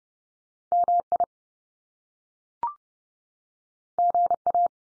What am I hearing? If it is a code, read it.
Morse code, heard as a keyed tone, about 30 wpm, 700 Hz. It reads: MI GA
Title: US States 30WPM